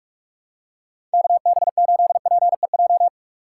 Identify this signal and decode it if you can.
Morse code, heard as a keyed tone, about 44 wpm, 700 Hz. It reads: KB8PEJ